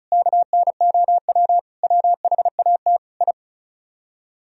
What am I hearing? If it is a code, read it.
Morse code, heard as a keyed tone, about 46 wpm, spaced as for 41 wpm, 700 Hz. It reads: KNOW WHAT I
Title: Speed-Racing - Sets of 4 Words 35wpm